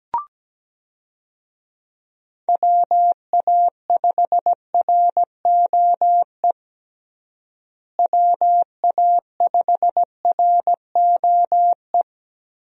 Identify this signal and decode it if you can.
Morse code, heard as a keyed tone, about 17 wpm, 700 Hz. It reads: WA5ROE WA5ROE
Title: USA Call Signs - 6 Characters - Repeated 3x - 17wpm